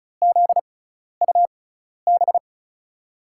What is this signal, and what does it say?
Morse code, heard as a keyed tone, about 36 wpm, spaced as for 12 wpm, 700 Hz. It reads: ZUB
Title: USA Call Signs - 5 Characters - 6x Character Spacing - 35wpm